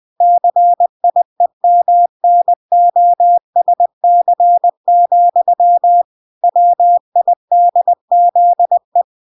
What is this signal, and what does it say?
Morse code, heard as a keyed tone, about 20 wpm, 700 Hz. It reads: CIEMNOSC, WIDZE